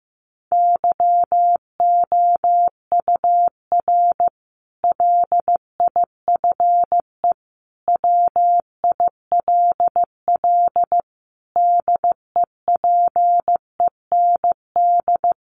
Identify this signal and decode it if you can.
Morse code, heard as a keyed tone, about 15 wpm, 700 Hz. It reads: YOUR LIFE WILL DEPEND
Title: Sentences from Top 1300 Words 15wpm